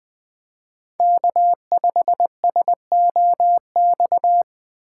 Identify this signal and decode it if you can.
Morse code, heard as a keyed tone, about 21 wpm, 700 Hz. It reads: K5SOX